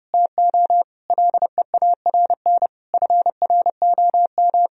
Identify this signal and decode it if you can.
Morse code, heard as a keyed tone, about 32 wpm, 700 Hz. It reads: TO LEARN FROM
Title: Sentences from Top 1000 Words - Review Part 2 - 30wpm